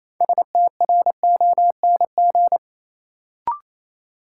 Morse code, heard as a keyed tone, about 28 wpm, 700 Hz. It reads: STRONG
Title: Sentences from Top 500 Words - New Word Drill - 28wpm